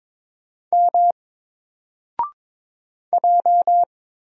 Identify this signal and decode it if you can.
Morse code, heard as a keyed tone, about 22 wpm, 700 Hz. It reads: M J
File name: Single-Letter-Number-22wpm